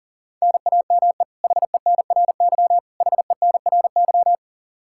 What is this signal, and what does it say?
Morse code, heard as a keyed tone, about 37 wpm, 700 Hz. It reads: NAME HENRY HENRY
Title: Practice QSO - Exchange 1 - Beginning - 40wpm